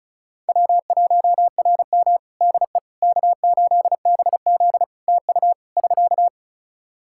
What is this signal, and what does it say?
Morse code, heard as a keyed tone, about 34 wpm, 700 Hz. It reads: W1RM DE K8BZ TU <SK>
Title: Practice QSO - Exchange 4 - Ending - 35wpm